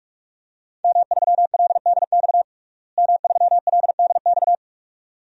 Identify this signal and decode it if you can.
Morse code, heard as a keyed tone, about 45 wpm, 700 Hz. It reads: M3LDX M3LDX
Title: DXCC-100 Call Signs - 5 Characters - Repeated 2x - 45wpm